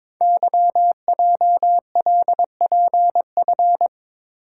Morse code, heard as a keyed tone, about 22 wpm, 700 Hz. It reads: YJLPF
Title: Code Groups - 5 Letters - 22wpm